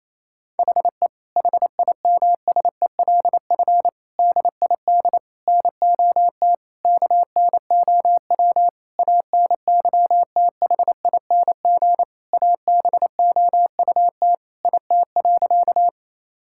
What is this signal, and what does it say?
Morse code, heard as a keyed tone, about 28 wpm, 700 Hz. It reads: HE HIMSELF DID NOT KNOW ANYTHING ABOUT IT.
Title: Sentences from Top 1600 Words 28wpm